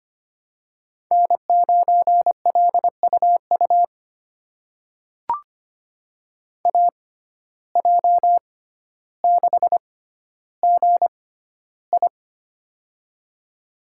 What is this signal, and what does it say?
Morse code, heard as a keyed tone, about 25 wpm, spaced as for 13 wpm, 700 Hz. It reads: N9LUU AJ6GI
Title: USA Call Signs - 5 Characters - 6x Character Spacing - 25wpm